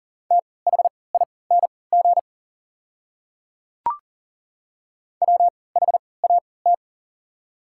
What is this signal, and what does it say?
Morse code, heard as a keyed tone, about 40 wpm, spaced as for 23 wpm, 700 Hz. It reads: THING WHAT
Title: 3x Character Spacing - Top 200 Words - 40wpm